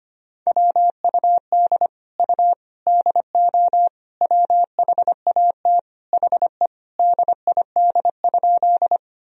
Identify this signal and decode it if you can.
Morse code, heard as a keyed tone, about 25 wpm, 700 Hz. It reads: WUD U DO WHAT HE DID?
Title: Sentences from Top 100 Words and Abbreviations 25wpm